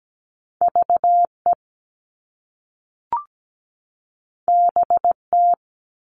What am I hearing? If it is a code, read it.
Morse code, heard as a keyed tone, about 17 wpm, 700 Hz. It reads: VE BT
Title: All Cut Numbers - 2-Digit 17wpm